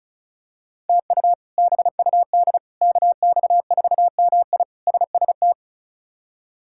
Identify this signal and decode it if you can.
Morse code, heard as a keyed tone, about 36 wpm, 700 Hz. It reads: TU BUD KX4MI SST
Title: K1USN SST QSO Practice - 35wpm